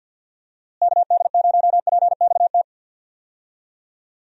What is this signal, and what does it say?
Morse code, heard as a keyed tone, about 45 wpm, 700 Hz. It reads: KD0PXT
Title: USA Call Signs - 6 Characters - Repeated 2x - 50wpm